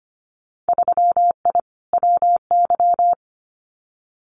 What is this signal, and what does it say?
Morse code, heard as a keyed tone, about 25 wpm, 700 Hz. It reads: 3I WY
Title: Winter Field Day Exchanges Encore 25wpm